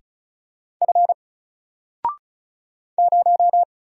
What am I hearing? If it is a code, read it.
Morse code, heard as a keyed tone, about 33 wpm, 700 Hz. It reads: F 0
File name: Single-Letter-Number-35wpm